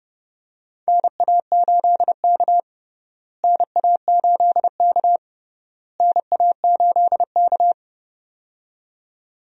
Morse code, heard as a keyed tone, about 30 wpm, 700 Hz. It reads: NA8K NA8K NA8K
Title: USA Call Signs - 4 Characters - Repeated 3x - 30wpm